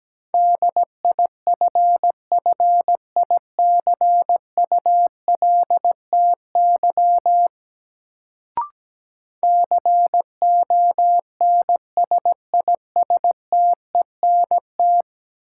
Morse code, heard as a keyed tone, about 17 wpm, 700 Hz. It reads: DIFFICULTY CONSISTENT
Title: Sentences from Top 1500 Words - New Word Drill - 17wpm